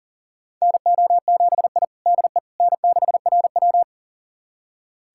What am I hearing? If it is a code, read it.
Morse code, heard as a keyed tone, about 40 wpm, 700 Hz. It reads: NO7I DE N6RW
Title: USA Callsign DE Callsign - 4 Characters - 40wpm